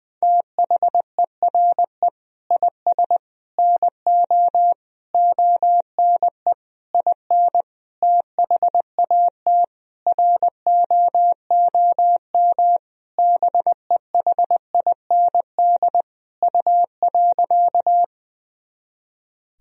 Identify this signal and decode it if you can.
Morse code, heard as a keyed tone, about 21 wpm, 700 Hz. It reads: THERE IS NO ONE IN THAT ROOM BEHIND U.
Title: Sentences from Top 500 Words and Abbreviations 20wpm